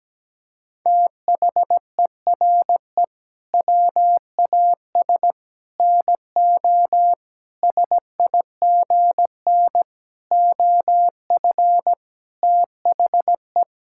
Morse code, heard as a keyed tone, about 17 wpm, 700 Hz. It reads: THERE WAS NO SIGN OF THE
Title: Sentences from Top 1000 Words - Review Part 6 - 17wpm